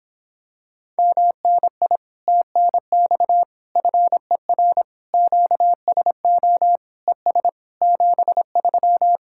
Morse code, heard as a keyed tone, about 26 wpm, 700 Hz. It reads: MNI TNX FER QSO ES 73